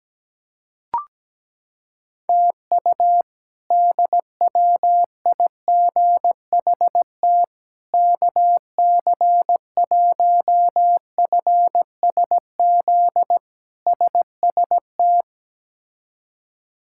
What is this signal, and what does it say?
Morse code, heard as a keyed tone, about 17 wpm, 700 Hz. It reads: TU DWIGHT KC1FSZ SST
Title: K1USN SST QSO Practice - 17wpm